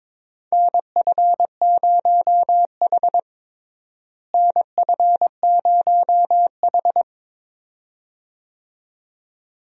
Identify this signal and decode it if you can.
Morse code, heard as a keyed tone, about 22 wpm, 700 Hz. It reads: NF0H NF0H
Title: USA Call Signs - 4 Characters - Repeated 2x - 22wpm